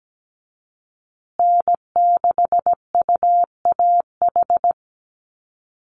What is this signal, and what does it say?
Morse code, heard as a keyed tone, about 17 wpm, 700 Hz. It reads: N6UAH
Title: USA Call Signs Repeated 2x 17wpm